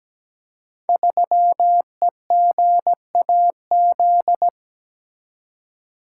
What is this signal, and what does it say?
Morse code, heard as a keyed tone, about 17 wpm, 700 Hz. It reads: 3EGAZ
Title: Code Groups - 5 Letter:Numbers - 17wpm